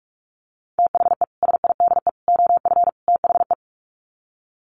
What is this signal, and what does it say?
Morse code, heard as a keyed tone, about 44 wpm, 700 Hz. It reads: THE SIDE OF THE
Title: Sets of 4 Words 45wpm